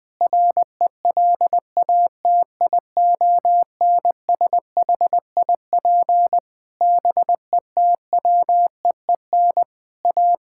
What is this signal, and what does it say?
Morse code, heard as a keyed tone, about 20 wpm, 700 Hz. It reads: RELATIONSHIP BETWEEN A
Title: Sentences with 6-Word Phrases - 20wpm